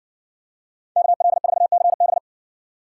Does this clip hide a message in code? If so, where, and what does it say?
Yes: Morse code from 1.0 to 2.2 s, heard as a keyed tone, about 56 wpm, 700 Hz. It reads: KL4CB